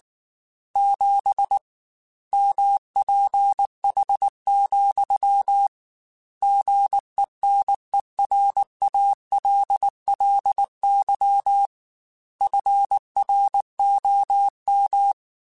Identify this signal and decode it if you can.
Morse code, heard as a keyed tone, about 19 wpm, 800 Hz. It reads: 7 MPH, GENERALLY FROM